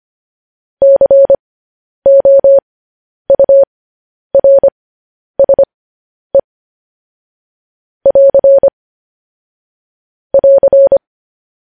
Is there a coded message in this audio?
Yes: Morse code from 0.8 to 11.0 s, heard as a keyed tone, about 25 wpm, spaced as for 10 wpm, 550 Hz. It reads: COURSE <AR> <AR>